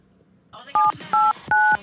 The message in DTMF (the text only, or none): 789